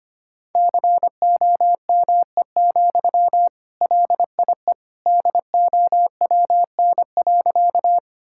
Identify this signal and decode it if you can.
Morse code, heard as a keyed tone, about 25 wpm, 700 Hz. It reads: COME, LIE DOWN.
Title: Sentences from Top 1600 Words 25wpm